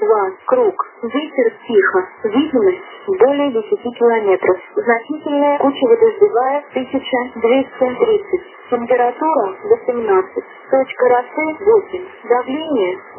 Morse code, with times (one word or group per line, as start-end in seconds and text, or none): none